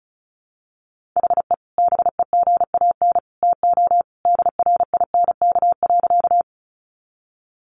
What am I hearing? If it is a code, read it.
Morse code, heard as a keyed tone, about 34 wpm, 700 Hz. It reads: HE BEGAN TO DRINK.
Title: Sentences from Top 1500 Words 35wpm